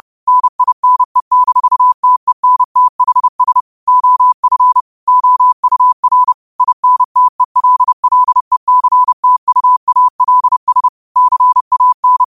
Morse code, heard as a keyed tone, about 30 wpm, 1000 Hz. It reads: NINE-TENTHS OF OUR INTELLECTUALS CAN
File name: qotdfiles / 30wpm